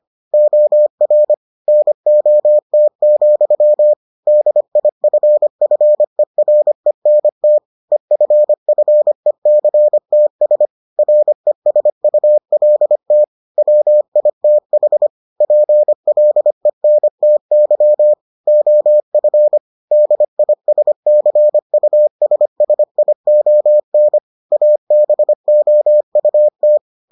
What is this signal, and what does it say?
Morse code, heard as a keyed tone, about 25 wpm, 600 Hz. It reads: OR NOT, DIFFERENT EFFECTS RESULT WITH PLENTY OF DISCUSSION ABOUT